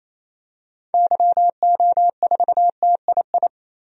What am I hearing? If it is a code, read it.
Morse code, heard as a keyed tone, about 28 wpm, 700 Hz. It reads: YO4TII
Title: DXCC-100 Call Signs - 6 Characters - 28wpm